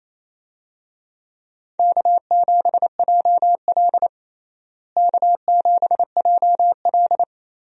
Morse code, heard as a keyed tone, about 28 wpm, 700 Hz. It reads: K7JL K7JL
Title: USA Call Signs - 4 Characters - Repeated 2x - 28wpm